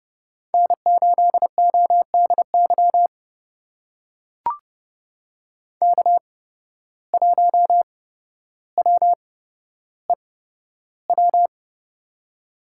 Morse code, heard as a keyed tone, about 30 wpm, spaced as for 13 wpm, 700 Hz. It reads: N8ODY K1WEW
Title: USA Call Signs - 5 Characters - 8x Character Spacing - 30wpm